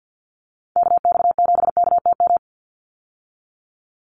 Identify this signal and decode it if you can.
Morse code, heard as a keyed tone, about 50 wpm, 700 Hz. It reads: KX7KTM
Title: USA Call Signs 50wpm